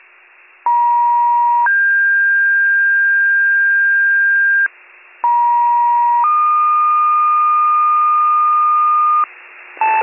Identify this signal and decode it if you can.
Two-tone page, heard as a one-second tone, then a three-second tone, about 950 then 1600 Hz, about 950 then 1220 Hz, with noise. It is heard from 0.7 to 9.2 s.